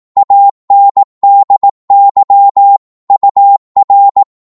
Morse code, heard as a keyed tone, about 18 wpm, 800 Hz. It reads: ANDY UR
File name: Morse / qso-18wpm